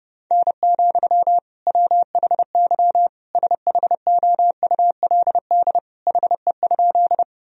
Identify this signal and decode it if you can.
Morse code, heard as a keyed tone, about 30 wpm, 700 Hz. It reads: N, WHY SHOULD HE?
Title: Sentences from Top 300 Words and Abbreviations 30wpm